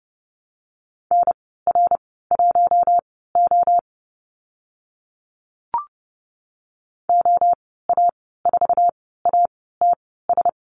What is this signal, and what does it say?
Morse code, heard as a keyed tone, about 30 wpm, spaced as for 17 wpm, 700 Hz. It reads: NR1O OA4ATS
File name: Lesson 14g - Number 2 - Call Signs 30wpm